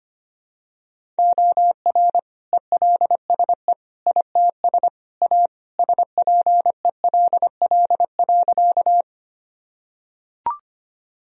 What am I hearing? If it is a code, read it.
Morse code, heard as a keyed tone, about 25 wpm, 700 Hz. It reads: OR ELSE ITS A SPELL.
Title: Sentences from Top 1000 Words - Review Part 1 - 25wpm